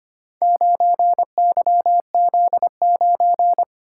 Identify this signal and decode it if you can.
Morse code, heard as a keyed tone, about 25 wpm, 700 Hz. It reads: 9YZ9
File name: practice / 25wpm / Four Letter-Number Combinations - 25wpm